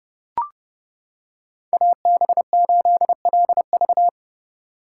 Morse code, heard as a keyed tone, about 29 wpm, 700 Hz. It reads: AB8LV